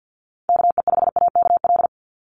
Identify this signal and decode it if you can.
Morse code, heard as a keyed tone, about 48 wpm, 700 Hz. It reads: KE5AKL
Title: NA SOTA Chaser Callsigns 50wpm